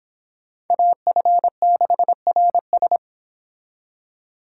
Morse code, heard as a keyed tone, about 25 wpm, 700 Hz. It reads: AF6RS